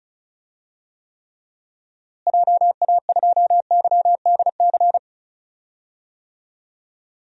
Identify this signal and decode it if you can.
Morse code, heard as a keyed tone, about 36 wpm, 700 Hz. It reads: JA2YDC
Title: DXCC-100 Call Signs - 6 Characters - Repeated 3x - 35wpm